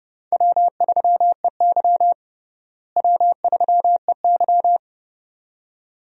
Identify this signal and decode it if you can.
Morse code, heard as a keyed tone, about 32 wpm, 700 Hz. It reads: W3EY W3EY
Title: USA Call Signs - 4 Characters - Repeated 2x - 30wpm